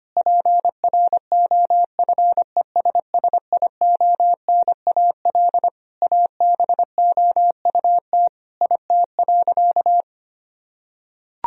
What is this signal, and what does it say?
Morse code, heard as a keyed tone, about 25 wpm, 700 Hz. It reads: PROFESSIONAL ABOUT IT.
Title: Sentences from Top 1000 Words - Review Part 4 - 25wpm